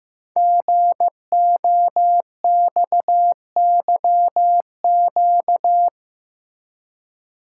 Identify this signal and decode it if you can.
Morse code, heard as a keyed tone, about 15 wpm, 700 Hz. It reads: GOXYQ
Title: Code Groups - 5 Letter:Numbers - 15wpm